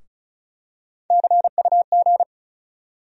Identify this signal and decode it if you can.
Morse code, heard as a keyed tone, about 34 wpm, 700 Hz. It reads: CUG
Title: USA Call Sign Postfixes - 3 Letters 35wpm